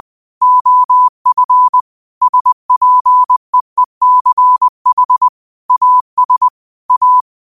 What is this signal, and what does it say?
Morse code, heard as a keyed tone, about 20 wpm, 1000 Hz. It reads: OF SPEECH AS A